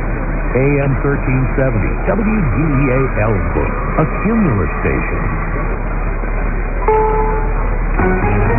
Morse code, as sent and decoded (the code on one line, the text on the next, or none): none